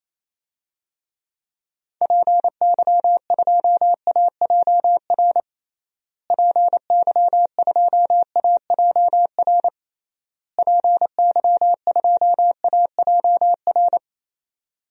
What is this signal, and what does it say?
Morse code, heard as a keyed tone, about 27 wpm, 700 Hz. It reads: PY2AJR PY2AJR PY2AJR
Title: DXCC-100 Call Signs - 6 Characters - Repeated 3x - 28wpm